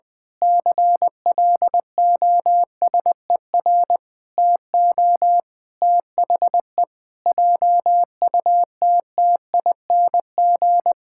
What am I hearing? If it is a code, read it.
Morse code, heard as a keyed tone, about 20 wpm, 700 Hz. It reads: CLOSER TO THE JUTTING
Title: Pangram Challenge v2 - 20wpm